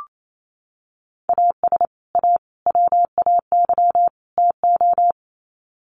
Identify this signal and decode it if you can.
Morse code, heard as a keyed tone, about 28 wpm, 700 Hz. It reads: AS A WAY TO
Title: Sets of 4 Words 28wpm